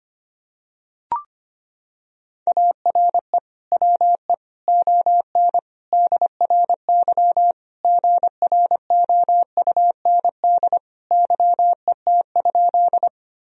Morse code, heard as a keyed tone, about 25 wpm, 700 Hz. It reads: ARE WE ON DRY GROUND YET?
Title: Sentences from Top 1000 Words - Review Part 3 - 25wpm